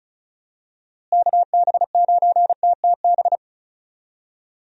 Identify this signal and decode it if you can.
Morse code, heard as a keyed tone, about 36 wpm, 700 Hz. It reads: KB9TTB
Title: USA Call Signs - 6 Characters - Repeated 2x - 35wpm